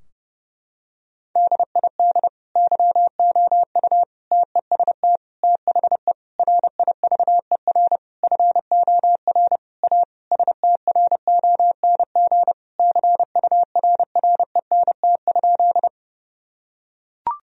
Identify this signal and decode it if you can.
Morse code, heard as a keyed tone, about 31 wpm, 700 Hz. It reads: DID YOU TEST THE RIVER FOR A STRONG CURRENT?
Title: Sentences from Top 1000 Words - Review Part 4 - 30wpm